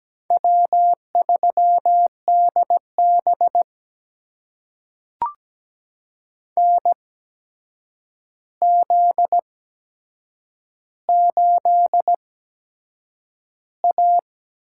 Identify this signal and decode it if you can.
Morse code, heard as a keyed tone, about 17 wpm, spaced as for 7 wpm, 700 Hz. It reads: W3DB NZ8A
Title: USA Call Signs - 8x Character Spacing - 17wpm